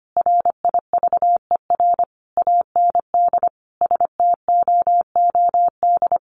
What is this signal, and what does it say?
Morse code, heard as a keyed tone, about 25 wpm, 700 Hz. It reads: RIVER AND STOOD
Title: Sentences from Top 1200 Words 25wpm